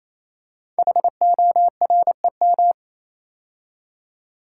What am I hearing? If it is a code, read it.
Morse code, heard as a keyed tone, about 28 wpm, 700 Hz. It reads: HOREM
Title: Code Groups - 5 Letters - 28wpm